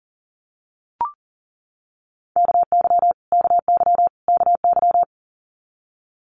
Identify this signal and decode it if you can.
Morse code, heard as a keyed tone, about 40 wpm, 700 Hz. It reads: KY KY KY
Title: Multi-letter Phonemes Repeated 3x 40wpm